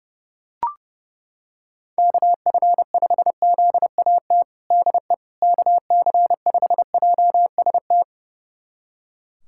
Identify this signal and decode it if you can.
Morse code, heard as a keyed tone, about 32 wpm, 700 Hz. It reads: KF5ZAT DE KC5JST
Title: USA Callsign DE Callsign - 6 Characters - 30wpm